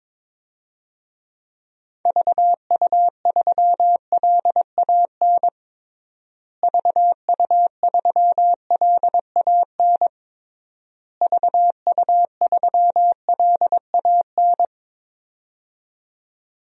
Morse code, heard as a keyed tone, about 22 wpm, 700 Hz. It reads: VU3LAN VU3LAN VU3LAN
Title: DXCC-100 Call Signs - 6 Characters - Repeated 3x - 22wpm